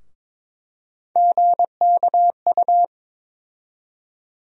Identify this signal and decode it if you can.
Morse code, heard as a keyed tone, about 22 wpm, 700 Hz. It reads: GKU
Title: USA Call Sign Postfixes - 3 Letters 22wpm